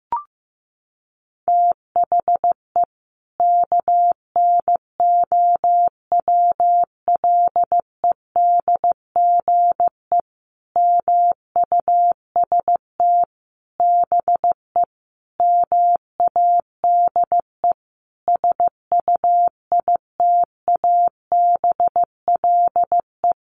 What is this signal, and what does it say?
Morse code, heard as a keyed tone, about 15 wpm, 700 Hz. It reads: THE KNOWLEDGE MUST BE MADE SUITABLE
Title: Sentences from Top 1400 Words 15wpm